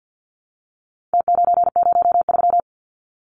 Morse code, heard as a keyed tone, about 50 wpm, 700 Hz. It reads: T903
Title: Numbers - Contest Style - 4-Digit 50wpm